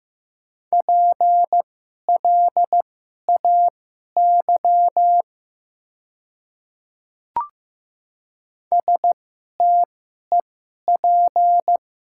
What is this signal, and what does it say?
Morse code, heard as a keyed tone, about 15 wpm, spaced as for 11 wpm, 700 Hz. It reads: PLAY STEP
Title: 2x Character Spacing - Top 500 Words - 15wpm